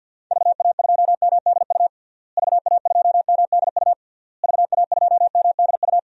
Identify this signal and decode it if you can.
Morse code, heard as a keyed tone, about 50 wpm, 700 Hz. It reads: VA2MDU VA2MDU VA2MDU
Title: DXCC-100 Call Signs - 6 Characters - Repeated 3x - 50wpm